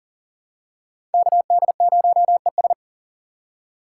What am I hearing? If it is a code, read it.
Morse code, heard as a keyed tone, about 40 wpm, 700 Hz. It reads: KD0ES
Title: USA Call Signs - 4x Character Spacing - 40wpm